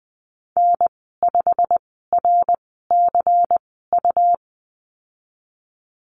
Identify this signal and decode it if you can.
Morse code, heard as a keyed tone, about 20 wpm, spaced as for 15 wpm, 700 Hz. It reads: N5RCU
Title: 179 Lesson 29g - Slash - Call Signs 20wpm